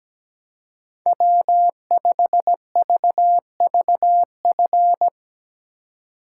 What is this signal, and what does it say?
Morse code, heard as a keyed tone, about 17 wpm, 700 Hz. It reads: W5VVF
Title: Speed Racing - USA 1x3 Call Signs 17wpm